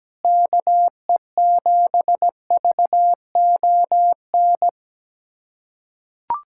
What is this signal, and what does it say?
Morse code, heard as a keyed tone, about 17 wpm, 700 Hz. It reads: KE7VON
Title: USA Callsign DE Callsign - 6 Characters - 17wpm